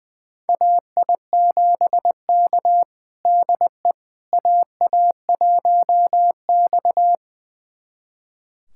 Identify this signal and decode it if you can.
Morse code, heard as a keyed tone, about 20 wpm, 700 Hz. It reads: AI7K DE AA1X
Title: USA Callsign DE Callsign - 4 Characters - 20wpm